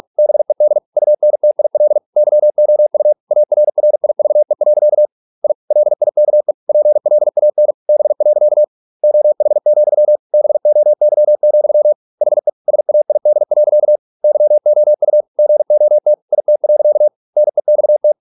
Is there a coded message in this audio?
Yes: Morse code from 0.2 to 18.1 s, heard as a keyed tone, about 46 wpm, 600 Hz. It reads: BED UNTIL YOU ARRIVE. I LIKE PLAN B. OH, BOY, HE SAID. YOU GOT IT. NEXT